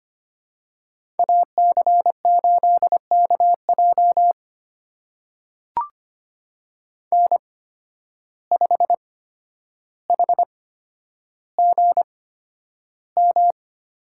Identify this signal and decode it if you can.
Morse code, heard as a keyed tone, about 25 wpm, spaced as for 11 wpm, 700 Hz. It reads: AC8KJ N5HGM
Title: USA Call Signs - 5 Characters - 8x Character Spacing - 25wpm